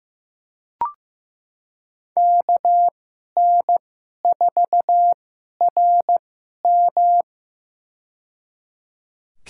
Morse code, heard as a keyed tone, about 16 wpm, spaced as for 11 wpm, 700 Hz. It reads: KN4RM
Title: USA Call Signs - 5 Characters - 2x Character Spacing - 15wpm